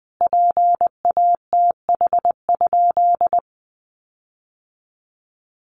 Morse code, heard as a keyed tone, about 20 wpm, 700 Hz. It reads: PATH?
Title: Sentences from Top 1400 Words 20wpm